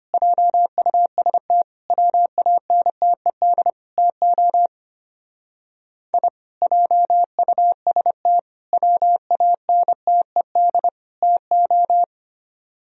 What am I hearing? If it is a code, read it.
Morse code, heard as a keyed tone, about 30 wpm, 700 Hz. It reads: JUST WANTED TO I JUST WANTED TO
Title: Speed-Racing - Sets of 4 Words 25wpm